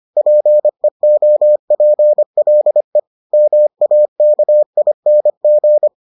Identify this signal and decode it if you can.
Morse code, heard as a keyed tone, about 25 wpm, 600 Hz. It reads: PEOPLE MAKING